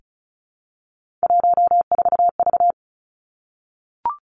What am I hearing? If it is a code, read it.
Morse code, heard as a keyed tone, about 36 wpm, 700 Hz. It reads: 14V
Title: Three Challenging Letter-Number Combinations 35wpm